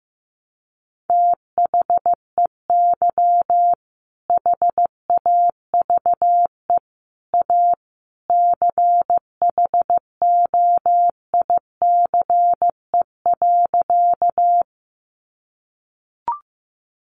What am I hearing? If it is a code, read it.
Morse code, heard as a keyed tone, about 14 wpm, 700 Hz. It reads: THEY HAVE A CHOICE.
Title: Sentences from Top 900 Words 15wpm